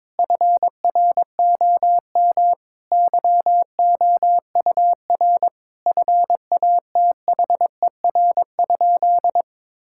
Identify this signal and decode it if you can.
Morse code, heard as a keyed tone, about 22 wpm, 700 Hz. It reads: FROM YOUR FATHER?
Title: Sentences from Top 1600 Words 22wpm